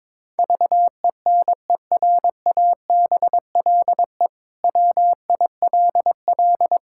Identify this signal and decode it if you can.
Morse code, heard as a keyed tone, about 22 wpm, 700 Hz. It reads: VENERABLE WILL
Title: Pangram Challenge v2 - 22wpm